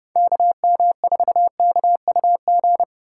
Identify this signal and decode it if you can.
Morse code, heard as a keyed tone, about 36 wpm, 700 Hz. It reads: KM4KUG
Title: Speed Racing - USA 2x3 Call Signs 30wpm